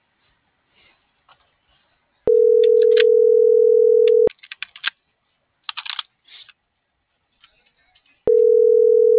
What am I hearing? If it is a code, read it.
Ringback tone (North America), 440 + 480 Hz.